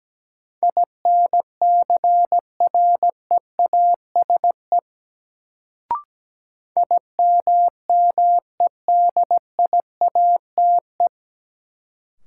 Morse code, heard as a keyed tone, about 17 wpm, 700 Hz. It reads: INCREASE IMMEDIATE
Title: Sentences from Top 1200 Words - New Word Drill Alt - 17wpm